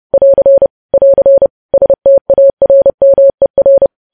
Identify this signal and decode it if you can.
Morse code, heard as a keyed tone, about 30 wpm, 550 Hz. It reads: <AR> <AR> STARMER
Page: News Headlines in Morse Code at 30 WPM